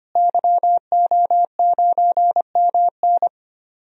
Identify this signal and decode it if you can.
Morse code, heard as a keyed tone, about 25 wpm, 700 Hz. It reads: YO9MN
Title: DXCC-100 Call Signs - 5 Characters - Repeated 3x - 25wpm